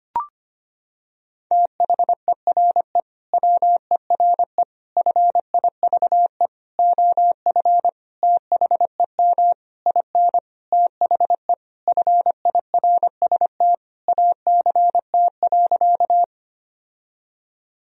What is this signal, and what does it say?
Morse code, heard as a keyed tone, about 25 wpm, 700 Hz. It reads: THERE WERE FIVE OF THEM IN THE FIRST ACT.
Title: Sentences from Top 1000 Words - Review Part 2 - 25wpm